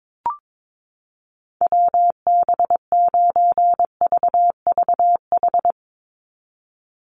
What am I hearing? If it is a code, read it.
Morse code, heard as a keyed tone, about 22 wpm, 700 Hz. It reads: WB9VVH
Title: USA Call Signs 22wpm